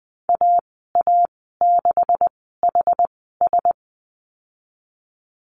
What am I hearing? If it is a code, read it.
Morse code, heard as a keyed tone, about 20 wpm, spaced as for 15 wpm, 700 Hz. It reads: AA6HS
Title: 128 Lesson 21g - Number 6 - Call Signs 20wpm